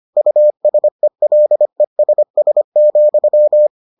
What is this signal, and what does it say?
Morse code, heard as a keyed tone, about 25 wpm, 600 Hz. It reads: USELESS,